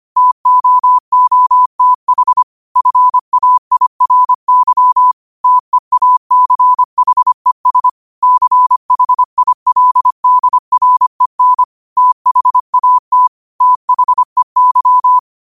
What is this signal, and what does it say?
Morse code, heard as a keyed tone, about 25 wpm, 1000 Hz. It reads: TOOTH FAIRY TEACHES CHILDREN THAT THEY
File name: qotdfiles / 25wpm